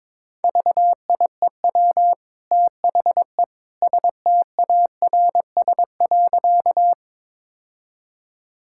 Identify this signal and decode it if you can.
Morse code, heard as a keyed tone, about 22 wpm, 700 Hz. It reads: VIEW THE STARS.
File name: practice / 22wpm / Easy Sentences with Hints - 22wpm